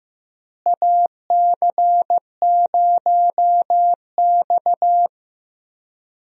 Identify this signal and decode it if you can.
Morse code, heard as a keyed tone, about 15 wpm, 700 Hz. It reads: AC0X
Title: USA Call Signs - 4 Characters - Repeated 3x - 15wpm